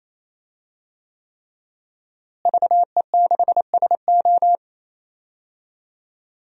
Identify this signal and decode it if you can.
Morse code, heard as a keyed tone, about 28 wpm, 700 Hz. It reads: VE6SO
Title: DXCC-100 Call Signs - 5 Characters - Repeated 3x - 28wpm